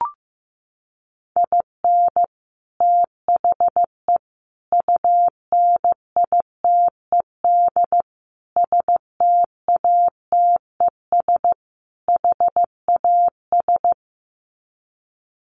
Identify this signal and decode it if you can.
Morse code, heard as a keyed tone, about 15 wpm, 700 Hz. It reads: IN THE UNITED STATES HAS
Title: Sets of 5 Words Encore2 15wpm